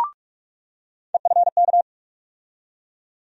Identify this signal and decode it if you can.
Morse code, heard as a keyed tone, about 46 wpm, 700 Hz. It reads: EFK
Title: USA Call Sign Postfixes - 3 Letters 45wpm